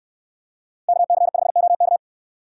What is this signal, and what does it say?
Morse code, heard as a keyed tone, about 69 wpm, 700 Hz. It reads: KC5OK